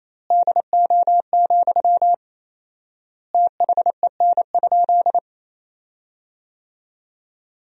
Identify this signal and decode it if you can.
Morse code, heard as a keyed tone, about 28 wpm, 700 Hz. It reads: DO, THEN?
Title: 4x Word-Spacing - Sentences from Top 100 Words - 28wpm